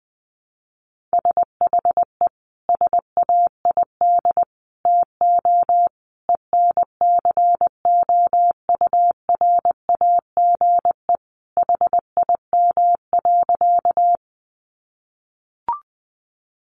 Morse code, heard as a keyed tone, about 20 wpm, 700 Hz. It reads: SHE SAID TO ENCOURAGE HIM.
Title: Sentences from Top 1400 Words 20wpm